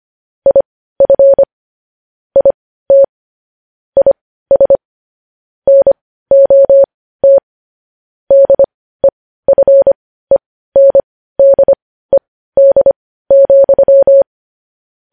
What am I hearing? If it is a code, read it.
Morse code, heard as a keyed tone, about 25 wpm, spaced as for 15 wpm, 550 Hz. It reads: IF IT IS NOT DEFENDED,
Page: news-headlines-in-morse-code-at-15-wpm